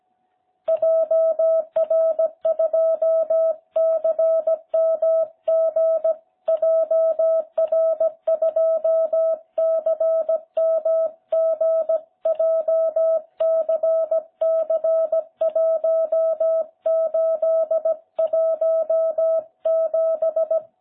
Morse code, heard as a keyed tone, about 17 wpm, 660 Hz. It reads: JR2CMG JR2CMG JCC1817